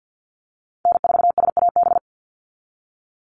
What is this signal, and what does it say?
Morse code, heard as a keyed tone, about 48 wpm, 700 Hz. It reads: N4SAB
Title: USA Call Signs 50wpm